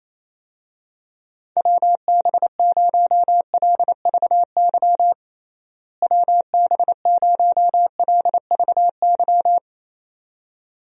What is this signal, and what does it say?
Morse code, heard as a keyed tone, about 28 wpm, 700 Hz. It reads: WB0LVY WB0LVY
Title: USA Call Signs - 6 Characters - Repeated 2x - 28wpm